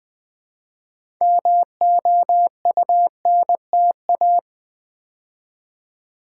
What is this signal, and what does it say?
Morse code, heard as a keyed tone, about 20 wpm, 700 Hz. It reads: MOUNTA
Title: Build A Word - 2 Letters At A Time - 8-Letter Words - 20wpm